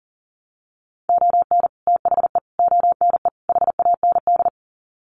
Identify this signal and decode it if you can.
Morse code, heard as a keyed tone, about 37 wpm, 700 Hz. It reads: ON THE ONE HAND
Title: Sets of 4 Words 40wpm